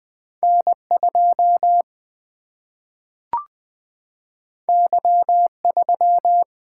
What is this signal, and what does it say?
Morse code, heard as a keyed tone, about 20 wpm, 700 Hz. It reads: N2 Y3
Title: Single-Letter Single-Number 20wpm